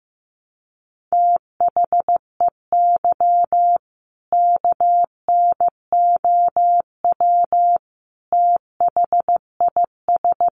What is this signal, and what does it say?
Morse code, heard as a keyed tone, about 14 wpm, 700 Hz. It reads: THEY KNOW THIS
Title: Sentences from Top 800 Words 15wpm